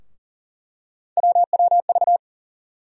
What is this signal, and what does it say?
Morse code, heard as a keyed tone, about 40 wpm, 700 Hz. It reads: WWV
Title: USA Call Sign Postfixes - 3 Letters 40wpm